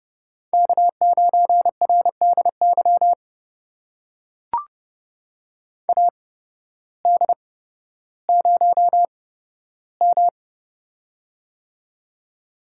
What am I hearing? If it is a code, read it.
Morse code, heard as a keyed tone, about 28 wpm, spaced as for 14 wpm, 700 Hz. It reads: K9RDY AD0M
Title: USA Call Signs - 8x Character Spacing - 30wpm